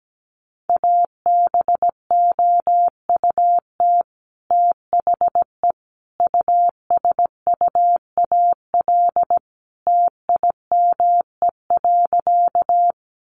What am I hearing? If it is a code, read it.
Morse code, heard as a keyed tone, about 17 wpm, 700 Hz. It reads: ABOUT THE USUAL TIME.
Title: Sentences from Top 400 Words 17wpm